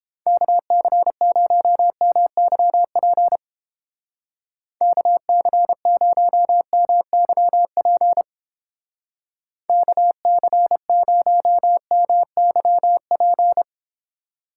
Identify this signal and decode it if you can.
Morse code, heard as a keyed tone, about 30 wpm, 700 Hz. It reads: KC0MYP KC0MYP KC0MYP